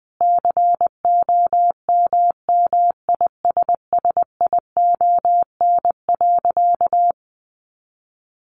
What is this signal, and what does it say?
Morse code, heard as a keyed tone, about 20 wpm, 700 Hz. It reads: COMMISSION.
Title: Sentences from Top 1400 Words 20wpm